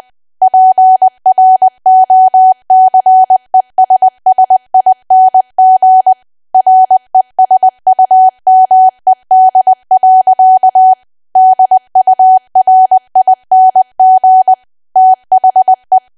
Morse code, heard as a keyed tone, about 20 wpm, 750 Hz. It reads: PROCESSING RESUMED. DURING THE
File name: cwa / W1AW-Level3 / 20WPM